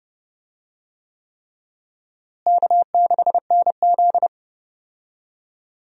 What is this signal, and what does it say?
Morse code, heard as a keyed tone, about 30 wpm, 700 Hz. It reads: K6NZ
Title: USA Call Signs - 4 Characters - Repeated 3x - 30wpm